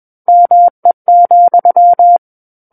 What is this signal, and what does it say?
Morse code, heard as a keyed tone, about 21 wpm, 700 Hz. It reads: ME,
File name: Book One / 12-Chapter 12 - 21 WPM